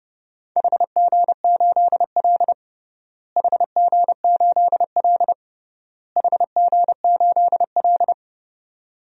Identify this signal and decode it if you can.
Morse code, heard as a keyed tone, about 29 wpm, 700 Hz. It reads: HG8L HG8L HG8L
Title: DXCC-100 Call Signs - 4 Characters - Repeated 3x - 30wpm